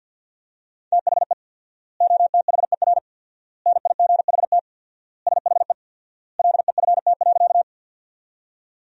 Morse code, heard as a keyed tone, about 48 wpm, 700 Hz. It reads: THE OTHER NIGHT SHE LEFT.
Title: 4x Word-Spacing - Sentences from Top 300 Words - 50wpm